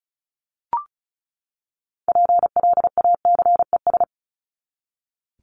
Morse code, heard as a keyed tone, about 34 wpm, 700 Hz. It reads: PLACES
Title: ES:S Suffix - Top 100 Words with Base Words 35wpm